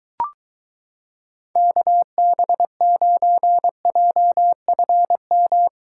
Morse code, heard as a keyed tone, about 24 wpm, 700 Hz. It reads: KB9JFM